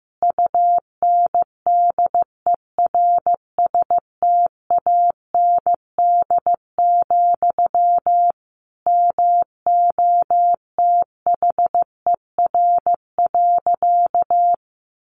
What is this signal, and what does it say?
Morse code, heard as a keyed tone, about 15 wpm, 700 Hz. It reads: UNDERSTAND, MOTHER.
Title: Sentences from Top 600 Words 15wpm